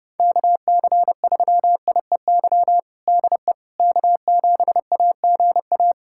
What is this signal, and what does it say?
Morse code, heard as a keyed tone, about 28 wpm, 700 Hz. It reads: KC3IEY DE K7AGA
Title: USA Callsign DE Callsign 30wpm